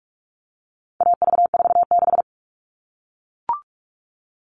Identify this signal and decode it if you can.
Morse code, heard as a keyed tone, about 44 wpm, 700 Hz. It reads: AV46
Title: All Cut Numbers - 4-Digit 45wpm